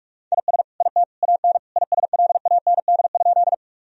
Morse code, heard as a keyed tone, about 46 wpm, 700 Hz. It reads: IS IT AN ISLAND?
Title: Sentences from Top 1000 Words - Review Part 3 - 45wpm